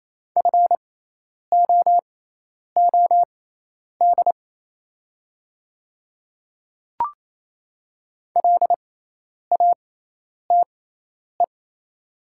Morse code, heard as a keyed tone, about 28 wpm, spaced as for 10 wpm, 700 Hz. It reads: FOOD LATE
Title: 6x Character Spacing - Top 300 Words - 28wpm